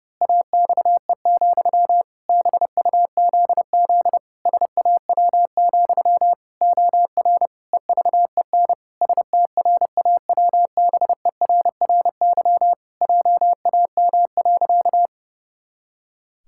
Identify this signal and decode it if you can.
Morse code, heard as a keyed tone, about 30 wpm, 700 Hz. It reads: AXE, BUZZ SAW, OR EVEN STRAWBERRY JAM.
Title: Pangram Challenge v2 - 30wpm